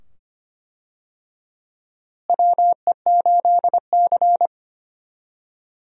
Morse code, heard as a keyed tone, about 25 wpm, 700 Hz. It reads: WE8C
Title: USA Call Signs - 4 Characters - Repeated 2x - 25wpm